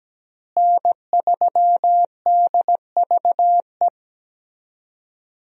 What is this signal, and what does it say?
Morse code, heard as a keyed tone, about 17 wpm, 700 Hz. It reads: N3DVE
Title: USA Call Signs - 5 Characters - Repeated 3x - 17wpm